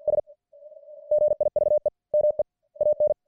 Morse code, heard as a keyed tone, about 50 wpm, 590 Hz. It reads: S GIVE ME AN